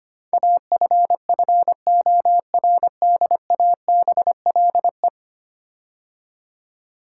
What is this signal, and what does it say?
Morse code, heard as a keyed tone, about 25 wpm, 690 Hz. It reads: AFFORDABLE
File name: 10 Letter Words - First 3 Letters Given - 25wpm